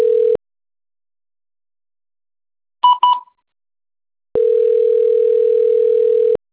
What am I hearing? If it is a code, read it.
Ringback tone (North America), 440 + 480 Hz.
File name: out- - - - - .26544